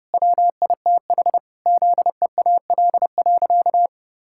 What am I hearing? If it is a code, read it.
Morse code, heard as a keyed tone, about 30 wpm, 700 Hz. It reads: WITH ZEAL.
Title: Pangram Challenge v2 - 30wpm